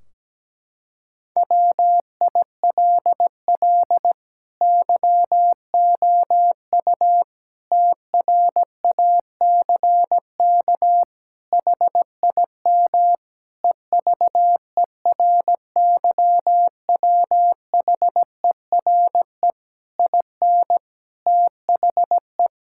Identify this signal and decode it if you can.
Morse code, heard as a keyed tone, about 17 wpm, 700 Hz. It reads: WILL YOU TRACK HIM EVERYWHERE IN THE
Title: Sentences from Top 1600 Words 17wpm